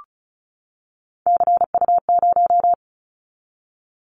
Morse code, heard as a keyed tone, about 36 wpm, 700 Hz. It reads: CU0
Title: Callsign Prefixes for Top 50 DXCC - 35wpm